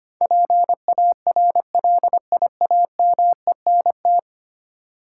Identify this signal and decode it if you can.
Morse code, heard as a keyed tone, about 25 wpm, 700 Hz. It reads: PARLIAMENT
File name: 10 Letter Words - First 3 Letters Given - 25wpm